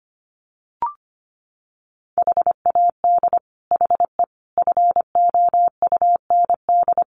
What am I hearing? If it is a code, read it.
Morse code, heard as a keyed tone, about 25 wpm, 700 Hz. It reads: HAD HE FOUND
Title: Sentences from Top 1300 Words 25wpm